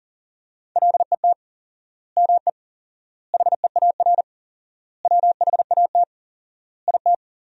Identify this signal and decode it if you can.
Morse code, heard as a keyed tone, about 40 wpm, 700 Hz. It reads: LET ME HEAR WHAT IT
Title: 4x Word-Spacing - Sentences from Top 300 Words - 40wpm